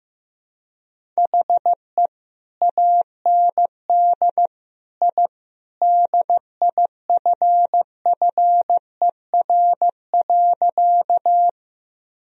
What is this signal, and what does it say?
Morse code, heard as a keyed tone, about 14 wpm, 700 Hz. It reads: HE AND I DIFFER.
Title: Sentences from Top 1000 Words - Review Part 1 - 15wpm